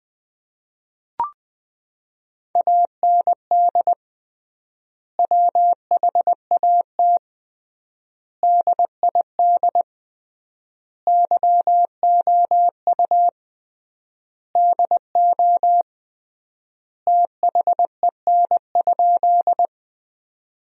Morse code, heard as a keyed tone, about 20 wpm, 700 Hz. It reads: AND WHAT DID YOU DO THEN?
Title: 3x Word-Spacing - Sentences from Top 100 Words - 20wpm